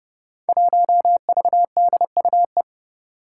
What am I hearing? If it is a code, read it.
Morse code, heard as a keyed tone, about 30 wpm, 700 Hz. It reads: 1VDUE
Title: Five Challenging Letter-Number Combinations 30wpm